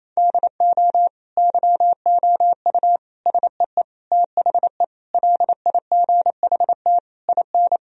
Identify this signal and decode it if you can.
Morse code, heard as a keyed tone, about 28 wpm, 700 Hz. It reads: DO YOU SEE THE LIGHT IN
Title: Sentences from Top 1000 Words - Review Part 1 - 28wpm